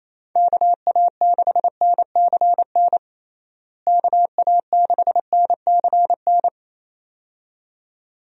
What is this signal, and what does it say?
Morse code, heard as a keyed tone, about 28 wpm, 700 Hz. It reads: KA6NCN KA6NCN
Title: USA Call Signs - 6 Characters - Repeated 2x - 28wpm